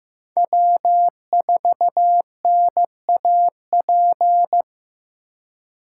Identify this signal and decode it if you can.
Morse code, heard as a keyed tone, about 18 wpm, 700 Hz. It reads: W4NAP
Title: Speed Racing - USA 1x3 Call Signs 15wpm